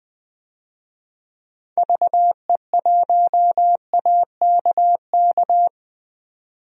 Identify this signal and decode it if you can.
Morse code, heard as a keyed tone, about 21 wpm, 700 Hz. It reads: VE1AKK
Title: DXCC-100 Call Signs - 6 Characters - 20wpm